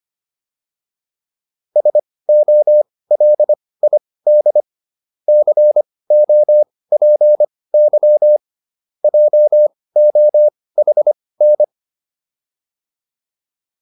Morse code, heard as a keyed tone, about 25 wpm, spaced as for 18 wpm, 600 Hz. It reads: SOLID COPY JOHN